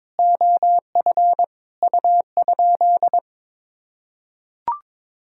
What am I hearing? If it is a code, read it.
Morse code, heard as a keyed tone, about 22 wpm, 700 Hz. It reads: OF U?
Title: Sentences from Top 300 Words and Abbreviations 22wpm